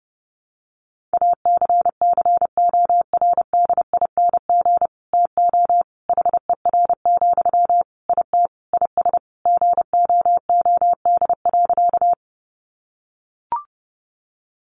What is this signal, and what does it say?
Morse code, heard as a keyed tone, about 28 wpm, 700 Hz. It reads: ACCORDING TO HER, IT IS GOOD.
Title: Sentences from Top 800 Words 30wpm